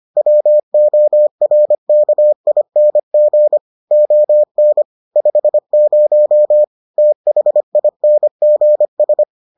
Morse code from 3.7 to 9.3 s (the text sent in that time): ON 50 THINGS